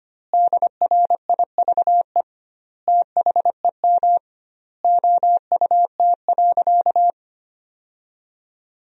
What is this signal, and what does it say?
Morse code, heard as a keyed tone, about 25 wpm, 700 Hz. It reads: DRIVE THEM OUT.
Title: 2x Word-Spacing - Sentences from Top 500 Words - 25wpm